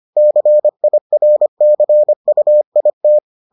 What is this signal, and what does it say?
Morse code, heard as a keyed tone, about 25 wpm, 600 Hz. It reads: CIRCUIT